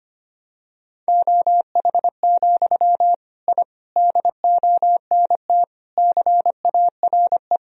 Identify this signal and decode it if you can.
Morse code, heard as a keyed tone, about 25 wpm, 700 Hz. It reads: OH, I DONT CARE
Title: Sentences from Top 1000 Words - Review Part 1 - 25wpm